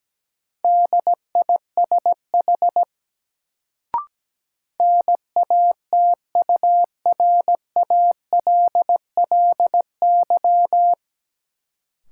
Morse code, heard as a keyed tone, about 17 wpm, 700 Hz. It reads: DISH NATURALLY
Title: Sentences from Top 1300 Words - New Word Drill Alt - 17wpm